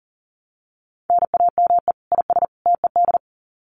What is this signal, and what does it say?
Morse code, heard as a keyed tone, about 40 wpm, 700 Hz. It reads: NAME IS TED
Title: Common QSO Phrases 40wpm